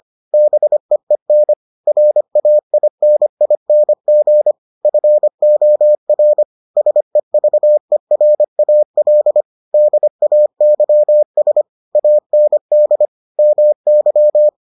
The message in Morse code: BEEN RAINING FOR SEVERAL DAYS AND MY

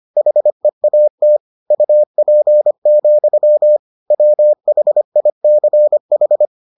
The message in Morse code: HEAT UP, WHICH

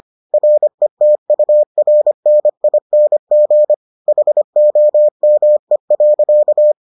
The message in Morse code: RETURNING HOME.

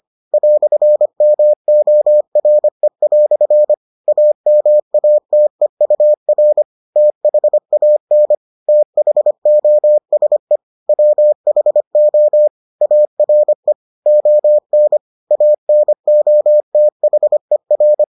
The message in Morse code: "MORE" AMATEUR THAN THOSE WHO ARE ON ANOTHER

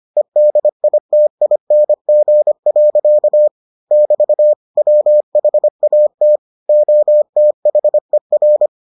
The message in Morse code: EDITING. <BT> WHAT OTHER

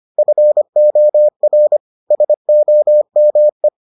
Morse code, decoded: FOR SOME

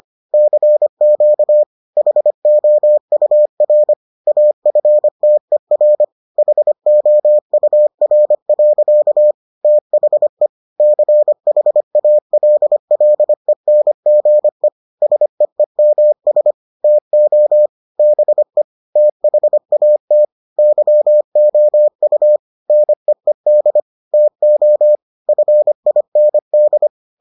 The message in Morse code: CQ HOUR AFTER HOUR. THE CHALLENGE SEEMS TO BE THAT YOU NEED TO FIND

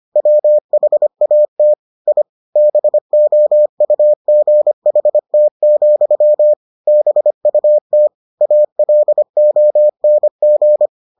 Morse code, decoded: WHAT I BOUGHT, BUT ALONG

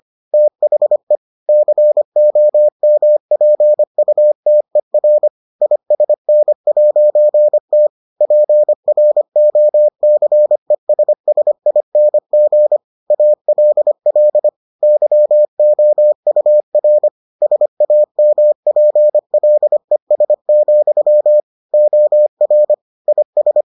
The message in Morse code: THE COMPUTER ISN'T PROCESSING ALL YOUR SAMPLES, OR IS

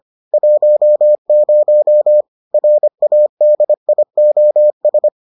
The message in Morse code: 10 RADIOS

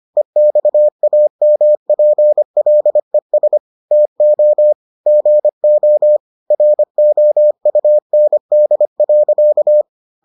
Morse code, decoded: EXAMPLES TO GO ROUND.

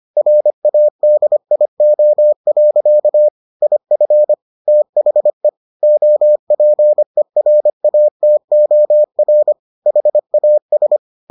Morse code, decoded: RADIO. IF THE OPERATOR HAS

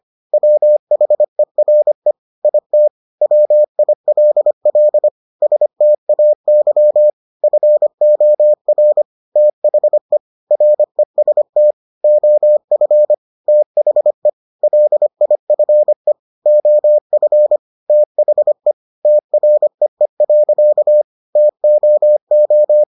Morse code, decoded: WHERE IT WILL STAY FOR THE REST OF THE LIFE OF THE TREE. TOO

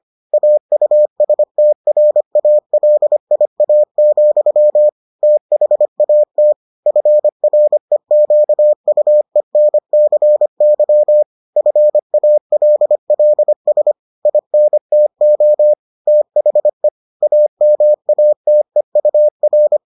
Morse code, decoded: AUSTRALIA, THAT FREQUENCY FALLS INTO THE AMATEUR

